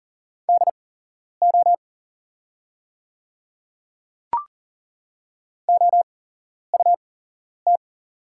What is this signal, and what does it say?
Morse code, heard as a keyed tone, about 40 wpm, spaced as for 11 wpm, 700 Hz. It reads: DO OUT
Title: 8x Character Spacing - Top 100 Words - 40wpm